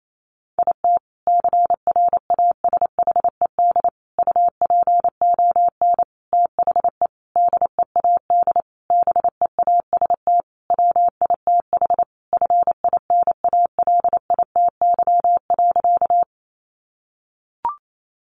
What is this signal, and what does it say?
Morse code, heard as a keyed tone, about 28 wpm, 700 Hz. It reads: IT CRASHED UPON THE DEAD BEAST WITH FINALITY.
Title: A Sound Of Thunder 28wpm